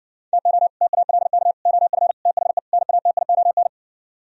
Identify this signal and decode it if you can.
Morse code, heard as a keyed tone, about 59 wpm, 700 Hz. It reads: TO TALK OF THE NATION